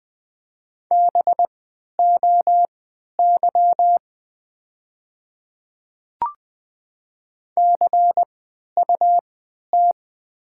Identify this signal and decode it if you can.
Morse code, heard as a keyed tone, about 20 wpm, spaced as for 11 wpm, 700 Hz. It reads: BOY CUT
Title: 3x Character Spacing - Top 500 Words - 20wpm